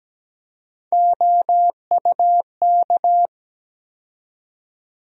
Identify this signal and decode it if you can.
Morse code, heard as a keyed tone, about 17 wpm, 700 Hz. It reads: OUK
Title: USA Call Sign Postfixes - 3 Letters 17wpm